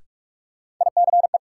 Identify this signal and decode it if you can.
Morse code, heard as a keyed tone, about 42 wpm, 700 Hz. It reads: ICE
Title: Sentences from Top 1000 Words - New Word Drill - 45wpm